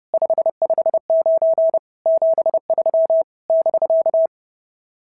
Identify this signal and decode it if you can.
Morse code, heard as a keyed tone, about 30 wpm, 650 Hz. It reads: 559 73 <BK>
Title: SOTA QSO Practice 30wpm